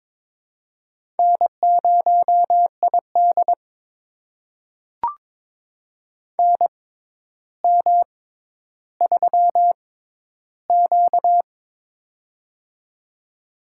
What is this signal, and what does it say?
Morse code, heard as a keyed tone, about 22 wpm, spaced as for 11 wpm, 700 Hz. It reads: N0ID NM3Q
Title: USA Call Signs - 4 Characters - 6x Character Spacing - 22wpm